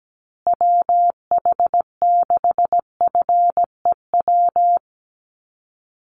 Morse code, heard as a keyed tone, about 17 wpm, 700 Hz. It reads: WH6FEW
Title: USA Call Signs Repeated 3x 17wpm